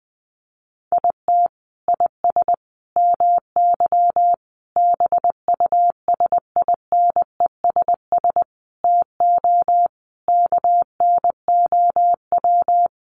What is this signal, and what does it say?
Morse code, heard as a keyed tone, about 20 wpm, 700 Hz. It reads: IT IS MY BUSINESS TO KNOW
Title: Sentences from Top 600 Words 20wpm